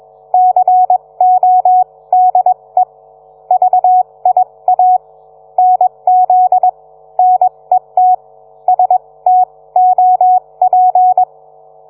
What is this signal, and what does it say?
Morse code, heard as a keyed tone, about 21 wpm, spaced as for 18 wpm, 740 Hz. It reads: CODE VIA NZ NET STOP